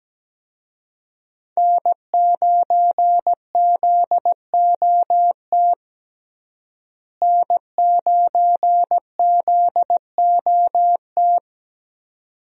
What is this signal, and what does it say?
Morse code, heard as a keyed tone, about 17 wpm, 700 Hz. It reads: N9ZOT N9ZOT
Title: USA Call Signs - 5 Characters - Repeated 2x - 17wpm